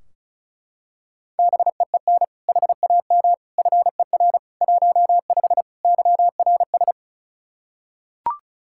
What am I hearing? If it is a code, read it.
Morse code, heard as a keyed tone, about 36 wpm, 700 Hz. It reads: BEEN HAM FER 15 YRS
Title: Practice QSO - Exchange 3 - More Info - 35wpm